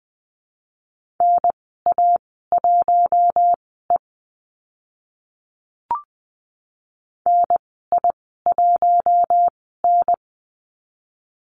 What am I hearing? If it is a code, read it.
Morse code, heard as a keyed tone, about 20 wpm, spaced as for 14 wpm, 700 Hz. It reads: NA1E NI1N
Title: 042 Lesson 8g - Number 1 - Call Signs 20wpm